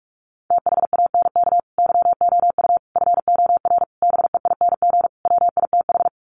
Morse code, heard as a keyed tone, about 44 wpm, 700 Hz. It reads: THANK YOU FOR BEING WITH
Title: Sets of 5 Words 45wpm